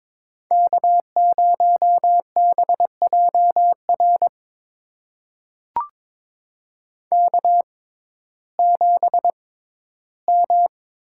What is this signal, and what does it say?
Morse code, heard as a keyed tone, about 22 wpm, spaced as for 14 wpm, 700 Hz. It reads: K0BJR K7M
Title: USA Call Signs - 5 Characters - 6x Character Spacing - 22wpm